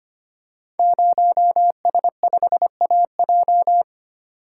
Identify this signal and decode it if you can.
Morse code, heard as a keyed tone, about 25 wpm, 700 Hz. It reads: 0S5AJ
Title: Code Groups - 5 Letter:Numbers - 25wpm